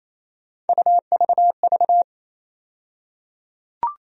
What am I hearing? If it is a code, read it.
Morse code, heard as a keyed tone, about 28 wpm, 700 Hz. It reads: UVV
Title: USA Call Sign Postfixes - 3 Letters 28wpm